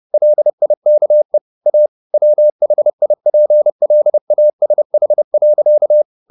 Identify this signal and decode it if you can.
Morse code, heard as a keyed tone, about 30 wpm, 600 Hz. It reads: LIKE A WHIPLASH.
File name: the old man and the sea / the old man and the sea 1 30wpm